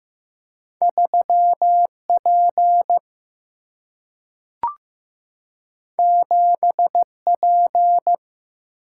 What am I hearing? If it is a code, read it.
Morse code, heard as a keyed tone, about 15 wpm, 700 Hz. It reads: 3 P 7 P
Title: Single-Letter Single-Number 15wpm